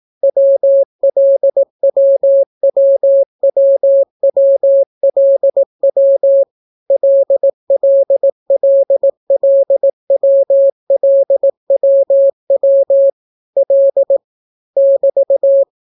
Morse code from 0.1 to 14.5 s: WLWWWWLW LLLLWLWW L